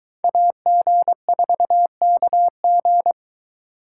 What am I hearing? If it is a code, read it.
Morse code, heard as a keyed tone, about 23 wpm, 700 Hz. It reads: AG4KG